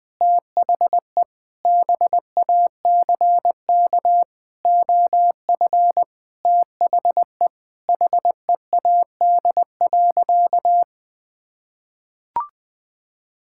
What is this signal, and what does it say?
Morse code, heard as a keyed tone, about 20 wpm, 700 Hz. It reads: THE BACK OF THE HEAD.
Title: Sentences with 6-Word Phrases - 20wpm